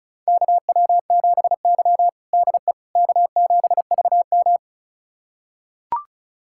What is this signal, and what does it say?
Morse code, heard as a keyed tone, about 36 wpm, 700 Hz. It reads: KW7Y DE K7VM
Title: USA Callsign DE Callsign - 4 Characters - 35wpm